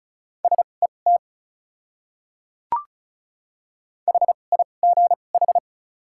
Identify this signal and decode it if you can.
Morse code, heard as a keyed tone, about 36 wpm, spaced as for 25 wpm, 700 Hz. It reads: SET HIGH
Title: 2x Character Spacing - Top 200 Words - 35wpm